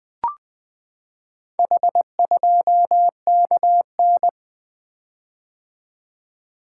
Morse code, heard as a keyed tone, about 20 wpm, 700 Hz. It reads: H2KN
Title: Four Letter-Number Combinations - 20wpm